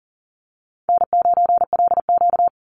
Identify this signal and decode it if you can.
Morse code, heard as a keyed tone, about 40 wpm, 700 Hz. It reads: N9LQ
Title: Easy Call Signs - 40wpm